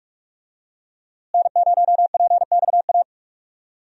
Morse code, heard as a keyed tone, about 43 wpm, 700 Hz. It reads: N0PXA
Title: USA Call Signs - 5 Characters - Repeated 3x - 45wpm